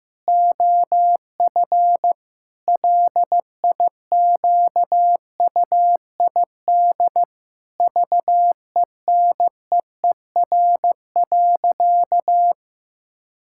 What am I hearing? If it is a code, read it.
Morse code, heard as a keyed tone, about 15 wpm, 700 Hz. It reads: OF LIQUID VENEER.
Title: Pangram Challenge v2 - 15wpm